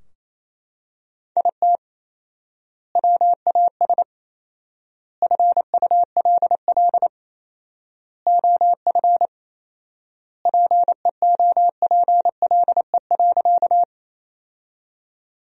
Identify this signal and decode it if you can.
Morse code, heard as a keyed tone, about 28 wpm, 700 Hz. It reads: IT WAS FULL OF PEOPLE.
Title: 4x Word-Spacing - Sentences from Top 500 Words - 28wpm